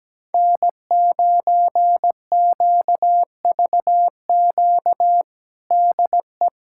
Morse code, heard as a keyed tone, about 17 wpm, 700 Hz. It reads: N9QVQ DE
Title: USA Callsign DE Callsign - 5 Characters - 17wpm